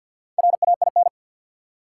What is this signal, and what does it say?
Morse code, heard as a keyed tone, about 52 wpm, 700 Hz. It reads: RAIN
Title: Four Letter Words - 50wpm